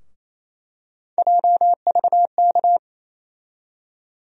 Morse code, heard as a keyed tone, about 28 wpm, 700 Hz. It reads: JVK
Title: USA Call Sign Postfixes - 3 Letters 28wpm